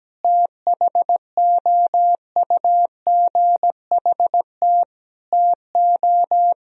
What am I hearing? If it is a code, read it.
Morse code, heard as a keyed tone, about 17 wpm, 700 Hz. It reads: THOUGHT TO
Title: Sentences from Top 1000 Words - Review Part 2 - 17wpm